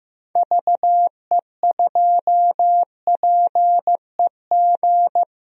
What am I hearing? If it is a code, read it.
Morse code, heard as a keyed tone, about 15 wpm, 700 Hz. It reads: VE2PEG
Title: DXCC-100 Call Signs - 6 Characters - 15wpm